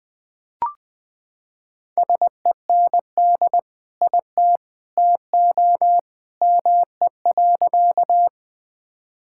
Morse code, heard as a keyed tone, about 21 wpm, 700 Hz. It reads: SEND IT TO ME.
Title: Sentences with 2-Word Phrases - 20wpm